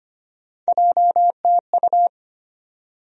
Morse code, heard as a keyed tone, about 25 wpm, 700 Hz. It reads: JTU
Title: USA Call Sign Postfixes - 3 Letters 25wpm